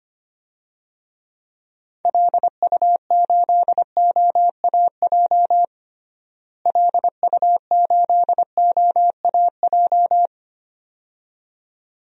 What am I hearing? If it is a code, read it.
Morse code, heard as a keyed tone, about 25 wpm, 700 Hz. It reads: LU8OAJ LU8OAJ
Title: DXCC-100 Call Signs - 6 Characters - Repeated 2x - 25wpm